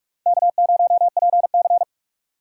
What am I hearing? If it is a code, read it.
Morse code, heard as a keyed tone, about 42 wpm, 700 Hz. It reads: K0PC
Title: K1USN SST QSO Practice - 45wpm